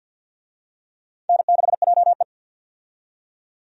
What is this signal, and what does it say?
Morse code, heard as a keyed tone, about 46 wpm, 700 Hz. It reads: N6JE
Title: USA Call Signs - 4 Characters - 50wpm